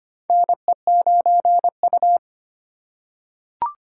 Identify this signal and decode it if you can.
Morse code, heard as a keyed tone, about 25 wpm, 700 Hz. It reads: NE9U
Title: K1USN SST QSO Practice - 25wpm